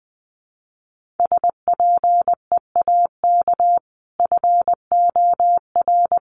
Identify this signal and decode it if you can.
Morse code, heard as a keyed tone, about 20 wpm, 700 Hz. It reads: SPEAK FOR
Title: Sentences from Top 1100 Words 20wpm